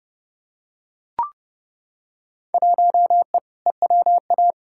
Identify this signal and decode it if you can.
Morse code, heard as a keyed tone, about 31 wpm, 700 Hz. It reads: 1E EWA
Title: Field Day Exchanges - 30wpm